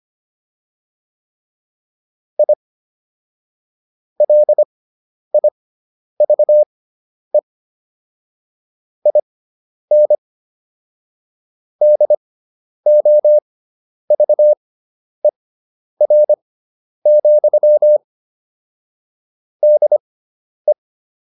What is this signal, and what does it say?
Morse code, heard as a keyed tone, about 25 wpm, spaced as for 10 wpm, 600 Hz. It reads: I LIVE IN DOVER, DE